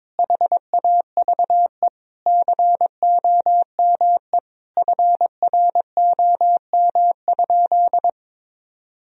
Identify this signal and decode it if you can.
Morse code, heard as a keyed tone, about 22 wpm, 700 Hz. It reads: HAVE COME FROM?
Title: Sentences from Top 1000 Words - Review Part 2 - 22wpm